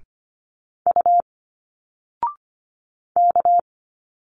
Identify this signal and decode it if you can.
Morse code, heard as a keyed tone, about 25 wpm, 700 Hz. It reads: U K
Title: Single Letters 25wpm